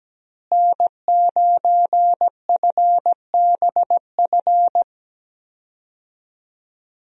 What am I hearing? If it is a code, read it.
Morse code, heard as a keyed tone, about 17 wpm, 700 Hz. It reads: N9FBF
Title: USA Call Signs - 5 Characters - Repeated 2x - 17wpm